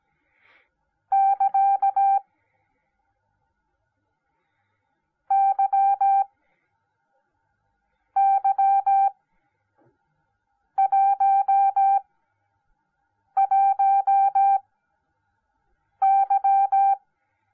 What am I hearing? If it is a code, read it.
Morse code, heard as a keyed tone, about 17 wpm, spaced as for 5 wpm, 790 Hz. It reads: <KA> YY11Y